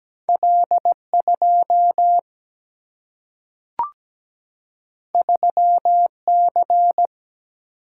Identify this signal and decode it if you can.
Morse code, heard as a keyed tone, about 17 wpm, 700 Hz. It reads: L2 3C
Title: Single-Letter Single-Number 17wpm